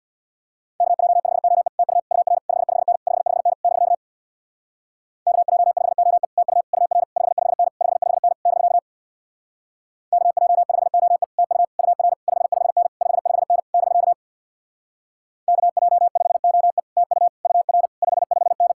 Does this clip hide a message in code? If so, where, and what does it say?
Yes: Morse code from 0.8 to 18.7 s, heard as a keyed tone, about 64 wpm, 700 Hz. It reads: KJ5OE TU UR 55N 55N <BK> KJ5OE TU UR 55N 55N <BK> KJ5OE TU UR 55N 55N <BK> KJ5OE TU UR 55N